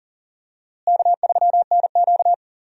Morse code, heard as a keyed tone, about 40 wpm, 700 Hz. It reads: K3NQ
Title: USA Call Signs - 4 Characters - Repeated 2x - 40wpm